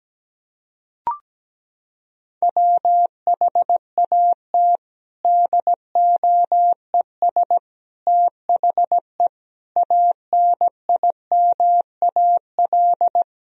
Morse code, heard as a keyed tone, about 17 wpm, 700 Hz. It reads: WHAT DOES THE ANIMAL
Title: Sentences from Top 1000 Words - Review Part 1 - 17wpm